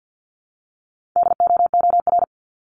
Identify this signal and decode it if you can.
Morse code, heard as a keyed tone, about 50 wpm, 700 Hz. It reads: DOOR
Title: Top 300 Words 50wpm